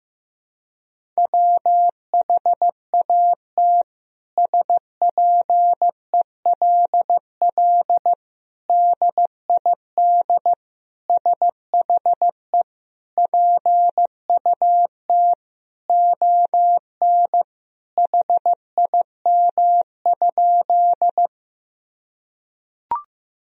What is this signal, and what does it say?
Morse code, heard as a keyed tone, about 15 wpm, 700 Hz. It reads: WHAT SPELL DID SHE PUT ON HIM?
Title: Sentences from Top 1000 Words - Review Part 1 - 15wpm